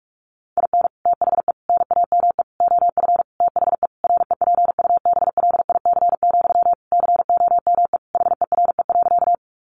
Morse code, heard as a keyed tone, about 44 wpm, 700 Hz. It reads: IN THE NAME OF THE REPUBLIC, COME HERE.
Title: Sentences from Top 1500 Words 45wpm